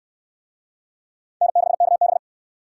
Morse code, heard as a keyed tone, about 65 wpm, 700 Hz. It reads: N6KB